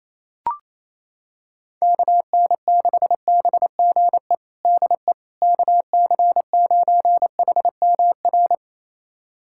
Morse code, heard as a keyed tone, about 28 wpm, 700 Hz. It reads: KN6BGE DE KC9HMR
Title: USA Callsign DE Callsign 28wpm